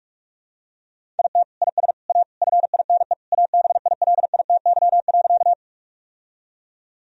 Morse code, heard as a keyed tone, about 46 wpm, 700 Hz. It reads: IT IS A FINE ABILITY.
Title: Sentences from Top 1000 Words - Review Part 3 - 45wpm